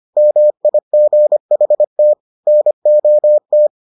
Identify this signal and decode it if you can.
Morse code, heard as a keyed tone, about 25 wpm, 600 Hz. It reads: MIGHT NOT